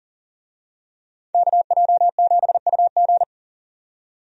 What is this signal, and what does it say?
Morse code, heard as a keyed tone, about 39 wpm, 700 Hz. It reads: KJ7UG
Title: USA Call Signs - 5 Characters - 40wpm